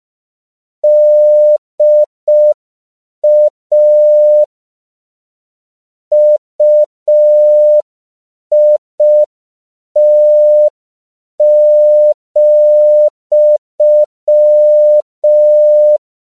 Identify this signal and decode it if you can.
Morse code, heard as a keyed tone, about 5 wpm, 600 Hz. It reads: DA UIT,
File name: CW / 5PPM